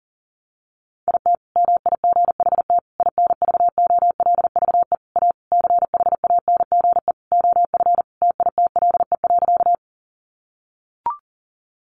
Morse code, heard as a keyed tone, about 40 wpm, 700 Hz. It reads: IT MIGHT INVOLVE A CHANGE OF TITLE.
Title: Sentences from Top 1200 Words 40wpm